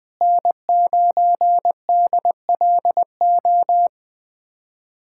Morse code, heard as a keyed tone, about 19 wpm, 700 Hz. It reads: N9DLO
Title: USA Callsign DE Callsign - 5 Characters - 20wpm